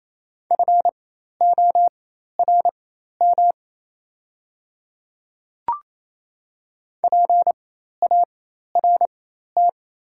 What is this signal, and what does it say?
Morse code, heard as a keyed tone, about 28 wpm, spaced as for 13 wpm, 690 Hz. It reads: FORM PART
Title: 4x Character Spacing - Top 300 Words - 28wpm